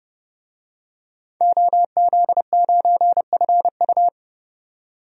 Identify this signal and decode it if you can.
Morse code, heard as a keyed tone, about 30 wpm, 700 Hz. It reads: OZ9FU
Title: DXCC-100 Call Signs - 5 Characters - 30wpm